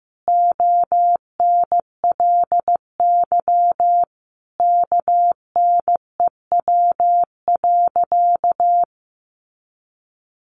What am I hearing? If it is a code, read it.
Morse code, heard as a keyed tone, about 15 wpm, 700 Hz. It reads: ONLY KNEW.
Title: Sentences from Top 500 Words 15wpm